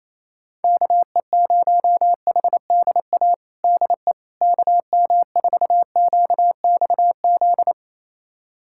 Morse code, heard as a keyed tone, about 27 wpm, 700 Hz. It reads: KE0HDA DE KM4QXZ
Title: USA Callsign DE Callsign - 6 Characters - 28wpm